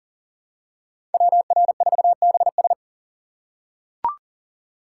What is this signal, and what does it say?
Morse code, heard as a keyed tone, about 37 wpm, 700 Hz. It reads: WR4BS
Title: USA Call Signs - 5 Characters - 2x Character Spacing - 40wpm